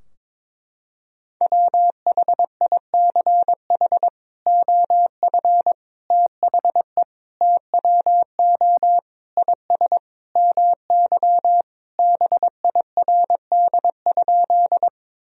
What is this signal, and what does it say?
Morse code, heard as a keyed tone, about 22 wpm, 700 Hz. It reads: WHICH OF THE TWO IS MY BIRD?
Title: Sentences from Top 1000 Words - Review Part 2 - 22wpm